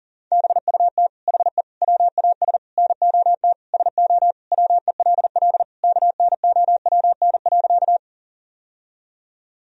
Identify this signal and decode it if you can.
Morse code, heard as a keyed tone, about 40 wpm, 700 Hz. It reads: BUT HE WAS NOT SO WELL KNOWN.
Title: Sentences from Top 1000 Words - Review Part 3 - 40wpm